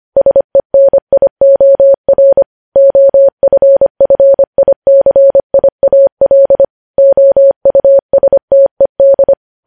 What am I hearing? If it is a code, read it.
Morse code, heard as a keyed tone, about 25 wpm, 550 Hz. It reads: SENIOR OFFICIAL OUSTED
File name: podcast / News headlines-1x-25x25